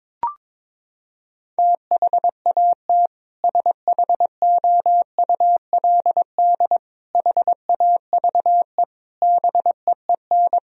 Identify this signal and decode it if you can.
Morse code, heard as a keyed tone, about 21 wpm, 700 Hz. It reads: THAT SHOULD HAVE BEEN
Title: Sentences from Top 1000 Words - Review Part 2 - 22wpm